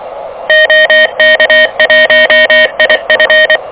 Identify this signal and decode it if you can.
Morse code, heard as a keyed tone, about 23 wpm, 590 Hz, with noise. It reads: OK1IF
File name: OK QRP 2023 / OK QRP 2023